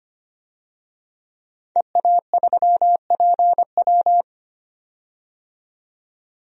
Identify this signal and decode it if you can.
Morse code, heard as a keyed tone, about 25 wpm, 700 Hz. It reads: EA3PW
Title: DXCC-100 Call Signs - 5 Characters - 25wpm